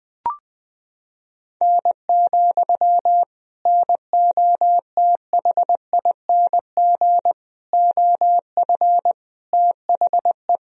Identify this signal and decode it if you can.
Morse code, heard as a keyed tone, about 21 wpm, 700 Hz. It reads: N, NOTHING OF THE
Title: Sentences from Top 500 Words and Abbreviations 20wpm